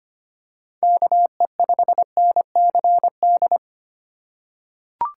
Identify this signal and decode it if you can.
Morse code, heard as a keyed tone, about 25 wpm, 700 Hz. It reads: KE5NCD
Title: USA Call Signs - 6 Characters - 2x Character Spacing - 25wpm